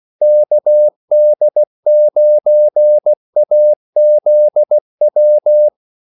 Morse code, heard as a keyed tone, about 16 wpm, 600 Hz. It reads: KD9AZW
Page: Kaldesignaler 16 wpm